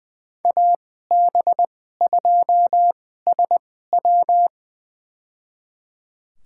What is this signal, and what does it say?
Morse code, heard as a keyed tone, about 20 wpm, spaced as for 14 wpm, 700 Hz. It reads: AB2SW
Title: USA Call Signs - 5 Characters - 2x Character Spacing - 20wpm